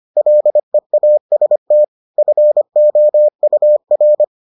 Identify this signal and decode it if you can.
Morse code, heard as a keyed tone, about 25 wpm, 600 Hz. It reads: LEAST FOUR